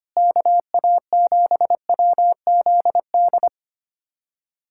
Morse code, heard as a keyed tone, about 25 wpm, 700 Hz. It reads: KA7WZD
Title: USA Callsign DE Callsign 25wpm